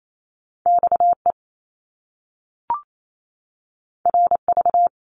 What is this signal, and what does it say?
Morse code, heard as a keyed tone, about 28 wpm, 700 Hz. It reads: XE RV
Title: Top Two Letter Combinations in Words 28wpm